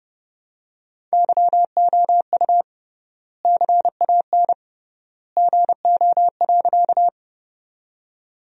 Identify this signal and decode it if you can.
Morse code, heard as a keyed tone, about 30 wpm, 700 Hz. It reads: YOU CAN GO.
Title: 3x Word-Spacing - Sentences from Top 100 Words - 30wpm